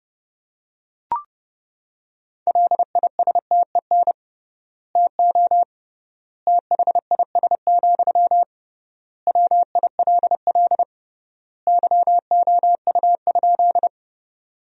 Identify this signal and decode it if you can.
Morse code, heard as a keyed tone, about 30 wpm, 700 Hz. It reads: LISTEN TO THIS, WILL YOU?
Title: 3x Word-Spacing - Sentences from Top 400 Words - 30wpm